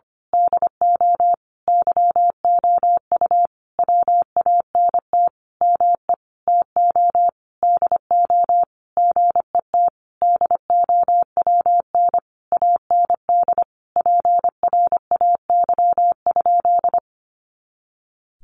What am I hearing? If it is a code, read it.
Morse code, heard as a keyed tone, about 25 wpm, 700 Hz. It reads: DO YOU WANT ME TO DO GET DOWN AND PRAY?
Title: A Sound Of Thunder 25wpm